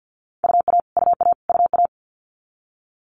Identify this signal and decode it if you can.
Morse code, heard as a keyed tone, about 50 wpm, 700 Hz. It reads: UA UA UA
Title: Multi-letter Phonemes Repeated 3x 50wpm